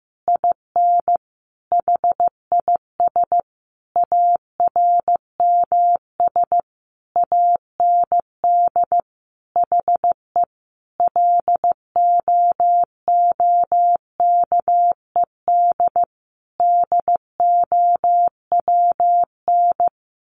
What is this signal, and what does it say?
Morse code, heard as a keyed tone, about 14 wpm, 700 Hz. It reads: IN HIS ARMS AND HE LOOKED DOWN
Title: A Sound Of Thunder 15wpm